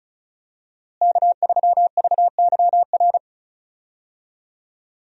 Morse code, heard as a keyed tone, about 34 wpm, 700 Hz. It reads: K3VYR
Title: USA Call Signs - 5 Characters - Repeated 3x - 35wpm